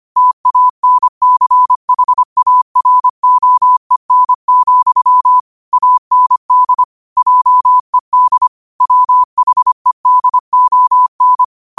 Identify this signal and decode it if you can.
Morse code, heard as a keyed tone, about 25 wpm, 1000 Hz. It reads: TANCHAROEN, AND JED WHEDON
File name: qotdfiles / 25wpm